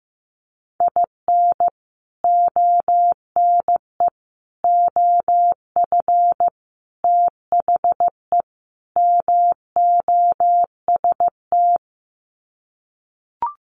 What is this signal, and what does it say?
Morse code, heard as a keyed tone, about 15 wpm, 700 Hz. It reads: IN ONE OF THE MOST
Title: Sets of 5 Words Encore 15wpm